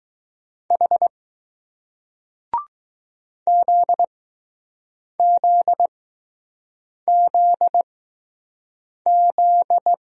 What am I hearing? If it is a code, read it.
Morse code, heard as a keyed tone, about 20 wpm, spaced as for 6 wpm, 700 Hz. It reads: H ZZZZ